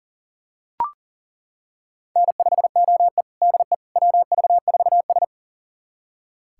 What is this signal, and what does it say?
Morse code, heard as a keyed tone, about 40 wpm, 700 Hz. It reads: N5OE DE WV4S
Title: USA Callsign DE Callsign - 4 Characters - 40wpm